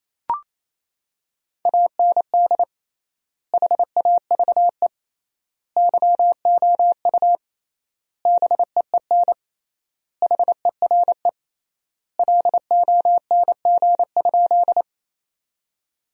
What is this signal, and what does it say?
Morse code, heard as a keyed tone, about 27 wpm, 700 Hz. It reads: AND HAVE YOU BEEN HERE LONG?
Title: 3x Word-Spacing - Sentences from Top 200 Words - 28wpm